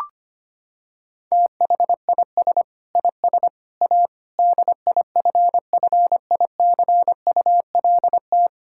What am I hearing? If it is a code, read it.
Morse code, heard as a keyed tone, about 25 wpm, 700 Hz. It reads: THIS IS A DIFFICULT
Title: Sentences from Top 1000 Words - Review Part 5 - 25wpm